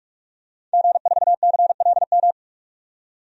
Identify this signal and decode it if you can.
Morse code, heard as a keyed tone, about 46 wpm, 690 Hz. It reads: G4CLM
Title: DXCC-100 Call Signs - 5 Characters - 45wpm